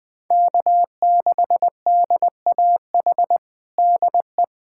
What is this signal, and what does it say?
Morse code, heard as a keyed tone, about 20 wpm, 700 Hz. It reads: K6DAH DE
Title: USA Callsign DE Callsign - 5 Characters - 20wpm